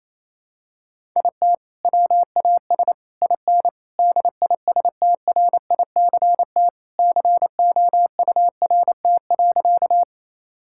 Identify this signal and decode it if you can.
Morse code, heard as a keyed tone, about 28 wpm, 700 Hz. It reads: IT WAS IN DISTRICT COURT.
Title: Sentences from Top 1600 Words 28wpm